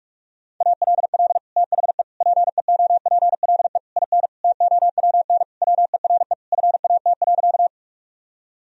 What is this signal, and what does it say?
Morse code, heard as a keyed tone, about 48 wpm, 700 Hz. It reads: ALL THE PEOPLE IN TOWN WERE FAT.
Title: Sentences from Top 1000 Words - Review Part 4 - 45wpm